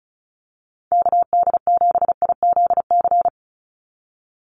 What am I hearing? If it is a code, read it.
Morse code, heard as a keyed tone, about 34 wpm, 700 Hz. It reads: KD7IZC
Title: USA Call Signs 35wpm